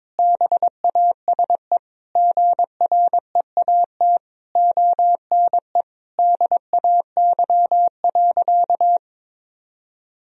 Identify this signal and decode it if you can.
Morse code, heard as a keyed tone, about 22 wpm, 700 Hz. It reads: BASE GREAT ONE DAY.
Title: Sentences from Top 1000 Words - Review Part 1 - 22wpm